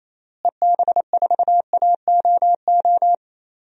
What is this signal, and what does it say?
Morse code, heard as a keyed tone, about 28 wpm, 700 Hz. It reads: EB4AOO
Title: DXCC-100 Call Signs - 6 Characters - Repeated 3x - 28wpm